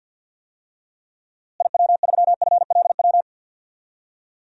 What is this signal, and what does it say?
Morse code, heard as a keyed tone, about 51 wpm, 700 Hz. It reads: IW3FLW
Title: DXCC-100 Call Signs - 6 Characters - Repeated 3x - 50wpm